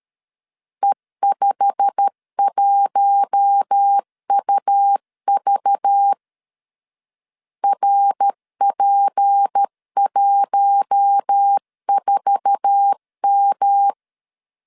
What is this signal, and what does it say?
Morse code, heard as a keyed tone, about 13 wpm, 800 Hz. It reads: E51UV RP14M
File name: LEZIONI / 8 YJ45 / 8 YJ45 12wpm 2151